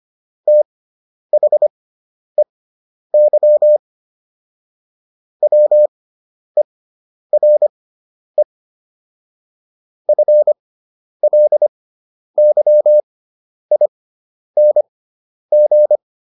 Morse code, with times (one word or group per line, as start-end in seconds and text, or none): 0.5-3.8 THEY
5.4-8.4 WERE
10.1-16.0 FLYING